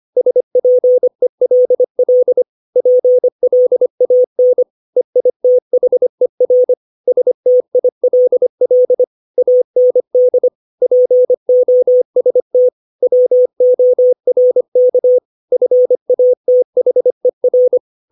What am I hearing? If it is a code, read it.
Morse code, heard as a keyed tone, about 25 wpm, 500 Hz. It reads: SPELL PLAN EITHER STILL AND POST WORK FATHER
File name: 632 words / 632 Words 25WPM